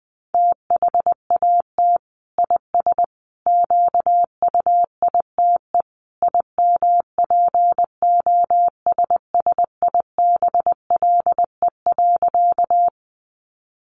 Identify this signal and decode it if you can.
Morse code, heard as a keyed tone, about 20 wpm, 700 Hz. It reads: THAT IS QUITE IMPOSSIBLE.
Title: Sentences from Top 1100 Words 20wpm